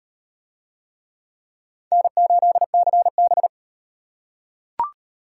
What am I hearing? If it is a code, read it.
Morse code, heard as a keyed tone, about 37 wpm, 700 Hz. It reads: N8CB